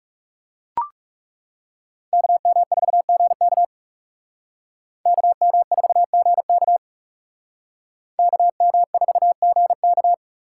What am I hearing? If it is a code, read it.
Morse code, heard as a keyed tone, about 36 wpm, 700 Hz. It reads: KM4GK KM4GK KM4GK
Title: Speed Racing - USA Call Signs 30wpm